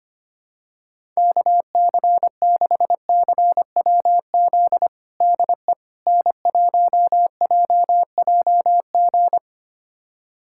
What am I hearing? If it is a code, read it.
Morse code, heard as a keyed tone, about 25 wpm, 700 Hz. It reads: KC6CWZ DE N1JJG
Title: USA Callsign DE Callsign 25wpm